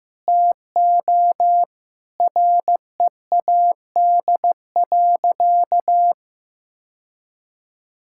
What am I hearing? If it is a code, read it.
Morse code, heard as a keyed tone, about 15 wpm, 700 Hz. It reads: TO READ.
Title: Sentences from Top 1000 Words - Review Part 5 - 15wpm